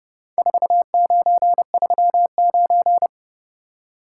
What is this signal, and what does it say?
Morse code, heard as a keyed tone, about 27 wpm, 700 Hz. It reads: 4939